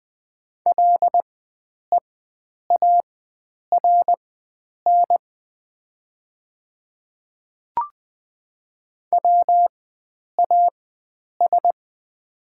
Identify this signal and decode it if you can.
Morse code, heard as a keyed tone, about 20 wpm, spaced as for 9 wpm, 690 Hz. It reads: LEARN WAS
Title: 4x Character Spacing - Top 300 Words - 20wpm